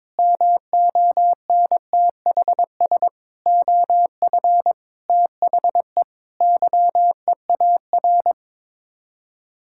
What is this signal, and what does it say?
Morse code, heard as a keyed tone, about 22 wpm, 700 Hz. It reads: MONTHS OF THE YEAR
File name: practice / 22wpm / Sets of 8 Words - 22wpm